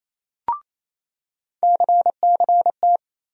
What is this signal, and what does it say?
Morse code, heard as a keyed tone, about 28 wpm, 700 Hz. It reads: CCT
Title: USA Call Sign Postfixes - 3 Letters 28wpm